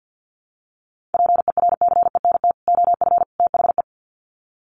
Morse code, heard as a keyed tone, about 50 wpm, 700 Hz. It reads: PERCENT OF THE
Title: Sets of 3 Words 50wpm